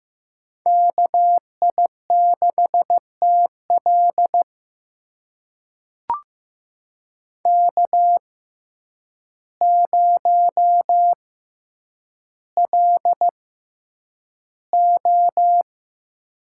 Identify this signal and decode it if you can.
Morse code, heard as a keyed tone, about 15 wpm, spaced as for 8 wpm, 700 Hz. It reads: KI6TL K0LO
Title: USA Call Signs - 6x Character Spacing - 15wpm